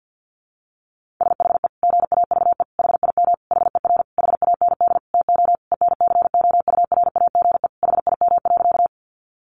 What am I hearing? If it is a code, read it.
Morse code, heard as a keyed tone, about 50 wpm, 700 Hz. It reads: SHE GAVE HIM HER HAND TO ENCOURAGE HIM.
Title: Sentences from Top 1400 Words 50wpm